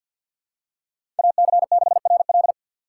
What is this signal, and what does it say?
Morse code, heard as a keyed tone, about 47 wpm, 700 Hz. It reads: AC6RL
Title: USA Call Signs - 5 Characters - 2x Character Spacing - 50wpm